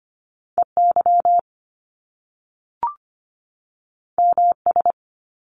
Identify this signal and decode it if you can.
Morse code, heard as a keyed tone, about 25 wpm, 700 Hz. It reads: EY MS
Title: Top Two Letter Combinations in Words 25wpm